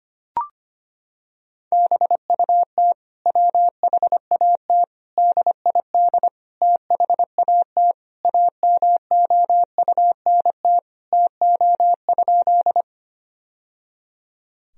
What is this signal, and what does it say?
Morse code, heard as a keyed tone, about 25 wpm, 700 Hz. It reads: BUT WHAT DID THAT AMOUNT TO?
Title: Sentences from Top 1000 Words - Review Part 4 - 25wpm